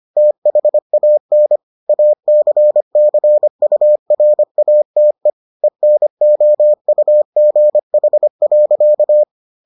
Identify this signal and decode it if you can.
Morse code, heard as a keyed tone, about 25 wpm, 600 Hz. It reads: THAN ACCURATE ENOUGH.